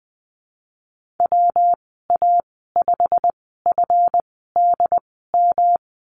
Morse code, heard as a keyed tone, about 20 wpm, spaced as for 15 wpm, 700 Hz. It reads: WA5FDM
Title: 147 Lesson 24g - Letter y - Call Signs 20wpm